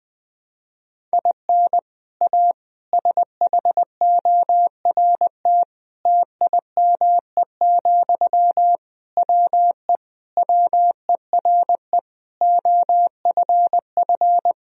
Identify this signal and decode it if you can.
Morse code, heard as a keyed tone, about 20 wpm, 700 Hz. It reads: IN A SHORT TIME, WE WERE OFF
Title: Sentences from Top 1000 Words - Review Part 2 - 20wpm